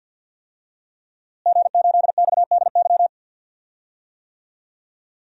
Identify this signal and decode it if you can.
Morse code, heard as a keyed tone, about 52 wpm, 700 Hz. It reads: G8XDY
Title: DXCC-100 Call Signs - 5 Characters - Repeated 3x - 50wpm